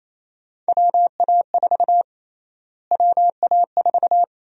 Morse code, heard as a keyed tone, about 28 wpm, 700 Hz. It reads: WA4 WA4
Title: USA Call Sign Prefixes - Repeated 2x - 28wpm